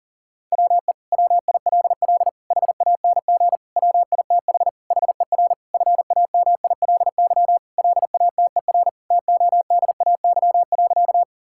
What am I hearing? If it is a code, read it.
Morse code, heard as a keyed tone, about 40 wpm, 700 Hz. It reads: WE WILL HANG WITH HER FAMILY LATER TODAY.
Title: Sentences from Top 1600 Words 40wpm